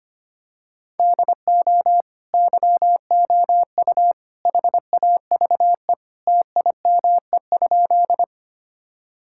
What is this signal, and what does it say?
Morse code, heard as a keyed tone, about 25 wpm, 700 Hz. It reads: DO YOU HAVE TIME?
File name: practice / 25wpm / Sentences with 3-Word Phrases - 25wpm